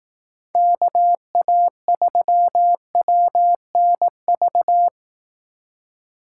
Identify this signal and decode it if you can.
Morse code, heard as a keyed tone, about 18 wpm, 700 Hz. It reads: KA3WNV
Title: Speed Racing - USA 2x3 Call Signs 15wpm